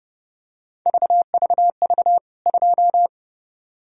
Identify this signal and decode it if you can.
Morse code, heard as a keyed tone, about 30 wpm, 700 Hz. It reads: VVV 2
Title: K1USN SST QSO Practice 30wpm